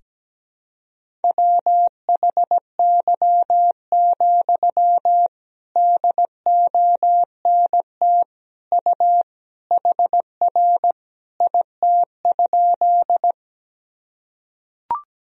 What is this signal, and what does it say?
Morse code, heard as a keyed tone, about 17 wpm, 700 Hz. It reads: WHY, DONT U HR IT?
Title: Sentences from Top 300 Words and Abbreviations 17wpm